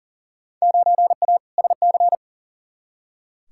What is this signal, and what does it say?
Morse code, heard as a keyed tone, about 40 wpm, 700 Hz. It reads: 9A SC
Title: Field Day Exchanges Encore - 40wpm